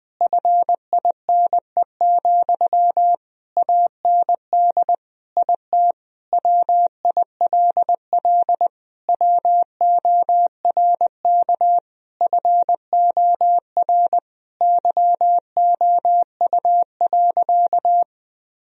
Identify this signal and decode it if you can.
Morse code, heard as a keyed tone, about 20 wpm, 700 Hz. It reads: FINE, AND IT WILL WORK FOR YOU.
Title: Sentences from Top 1000 Words - Review Part 2 - 20wpm